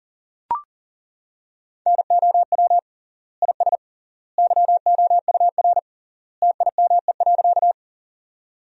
Morse code, heard as a keyed tone, about 40 wpm, 700 Hz. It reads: NOW IS YOUR TIME.
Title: 3x Word-Spacing - Sentences from Top 100 Words - 40wpm